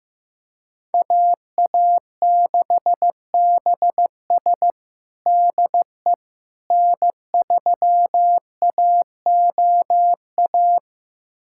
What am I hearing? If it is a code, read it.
Morse code, heard as a keyed tone, about 15 wpm, 700 Hz. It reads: AA6BS DE N3AOA
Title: USA Callsign DE Callsign - 5 Characters - 15wpm